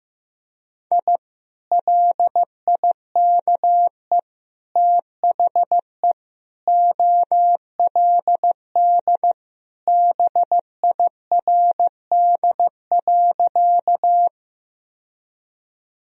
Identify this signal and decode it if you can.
Morse code, heard as a keyed tone, about 16 wpm, 700 Hz. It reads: I LIKE THE OLD BIRD.
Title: Sentences from Top 1000 Words - Review Part 2 - 15wpm